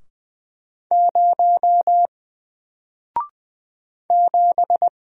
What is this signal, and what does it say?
Morse code, heard as a keyed tone, about 20 wpm, 700 Hz. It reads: 0 7
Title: Single Letter-Number 20wpm